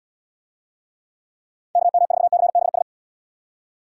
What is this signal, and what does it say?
Morse code, heard as a keyed tone, about 75 wpm, 700 Hz. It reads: KM4CBS